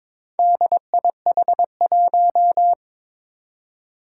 Morse code, heard as a keyed tone, about 21 wpm, 700 Hz. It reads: DIH1
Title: Four Challenging Letter-Number Combinations 22wpm